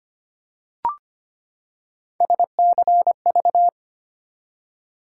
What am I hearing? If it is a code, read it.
Morse code, heard as a keyed tone, about 25 wpm, 700 Hz. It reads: SCV
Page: morse-code-ninja-practice-25wpm